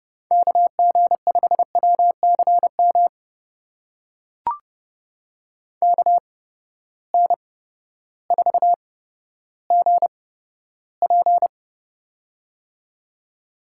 Morse code, heard as a keyed tone, about 30 wpm, spaced as for 14 wpm, 700 Hz. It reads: KG5WCM KN4GP
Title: USA Call Signs - 8x Character Spacing - 30wpm